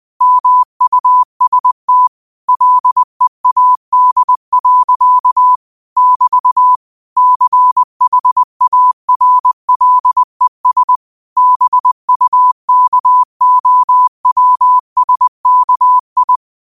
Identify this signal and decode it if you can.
Morse code, heard as a keyed tone, about 20 wpm, 1000 Hz. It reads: MUST LEAD. <BT> CHARLES BUKOWSKI
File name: qotdfiles / 20wpm